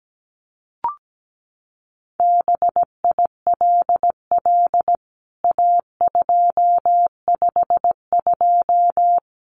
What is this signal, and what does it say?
Morse code, heard as a keyed tone, about 17 wpm, 700 Hz. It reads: BILL A252
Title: CWT Exchange 17wpm